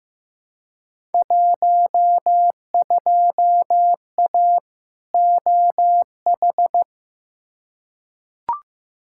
Morse code, heard as a keyed tone, about 15 wpm, 700 Hz. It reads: 12A OH
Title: Field Day Exchanges Encore - 15wpm